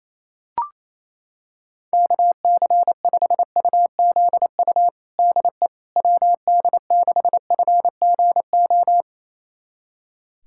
Morse code, heard as a keyed tone, about 27 wpm, 700 Hz. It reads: KC5UZU DE WD6FGO
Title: USA Callsign DE Callsign 28wpm